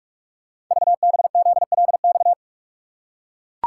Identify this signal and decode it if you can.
Morse code, heard as a keyed tone, about 44 wpm, 700 Hz. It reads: VBZLX
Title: Code Groups - 5 Letters - 45wpm